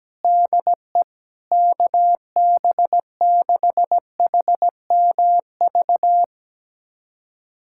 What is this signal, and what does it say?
Morse code, heard as a keyed tone, about 17 wpm, 700 Hz. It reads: DE KB6HMV
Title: USA Callsign DE Callsign - 6 Characters - 17wpm